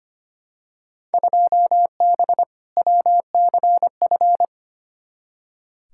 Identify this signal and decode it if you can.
Morse code, heard as a keyed tone, about 25 wpm, 700 Hz. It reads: 2B WCF
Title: Field Day Exchanges Encore - 25wpm